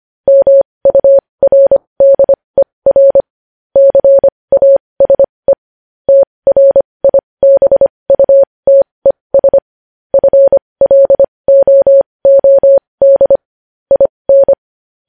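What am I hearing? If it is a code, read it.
Morse code, heard as a keyed tone, about 25 wpm, spaced as for 20 wpm, 550 Hz. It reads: MURDER CASE TRIBUTES FLOOD IN
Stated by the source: Morse code news headlines sent at 20 wpm.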